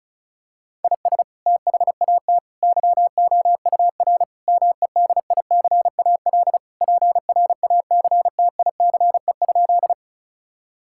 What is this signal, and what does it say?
Morse code, heard as a keyed tone, about 34 wpm, 700 Hz. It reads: IS THAT YOUR MEDICAL PRACTICE?
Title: Sentences from Top 1000 Words - Review Part 5 - 35wpm